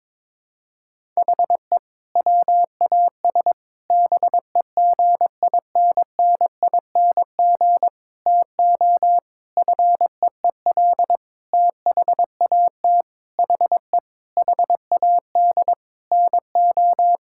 Morse code, heard as a keyed tone, about 22 wpm, 700 Hz. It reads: HE WAS BEGINNING TO FEEL THAT HE HAD NO
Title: Sentences from Top 1000 Words - Review Part 5 - 22wpm